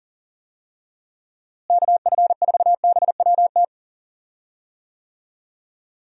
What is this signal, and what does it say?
Morse code, heard as a keyed tone, about 40 wpm, 700 Hz. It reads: KF4BWT
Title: USA Call Signs - 6 Characters - Repeated 3x - 40wpm